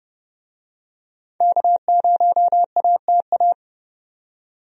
Morse code, heard as a keyed tone, about 30 wpm, 700 Hz. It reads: K0ATA
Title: USA Call Signs - 5 Characters - 30wpm